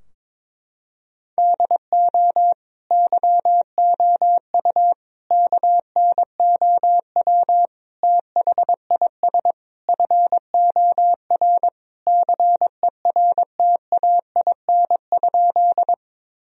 Morse code, heard as a keyed tone, about 22 wpm, 700 Hz. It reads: DO YOU KNOW THIS FOR CERTAIN?
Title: Sentences from Top 1000 Words - Review Part 2 - 22wpm